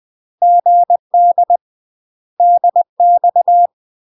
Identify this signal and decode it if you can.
Morse code, heard as a keyed tone, about 20 wpm, 700 Hz. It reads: GD DX
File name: UK MASTER QSOs TRAINING / 20WPM / UK MASTER QSOs 0062